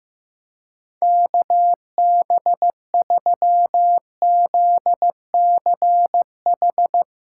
Morse code, heard as a keyed tone, about 15 wpm, 700 Hz. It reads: KB3ZCH
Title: USA Callsign DE Callsign 15wpm